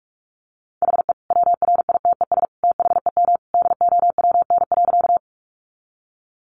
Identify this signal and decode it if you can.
Morse code, heard as a keyed tone, about 44 wpm, 700 Hz. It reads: HE WRITES THEM DOWN.
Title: Additional Easy Phrases 45wpm